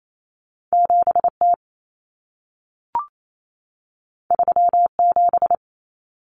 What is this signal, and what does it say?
Morse code, heard as a keyed tone, about 28 wpm, 700 Hz. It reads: 7T 37
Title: Common Cut Numbers - 2-Digit 28wpm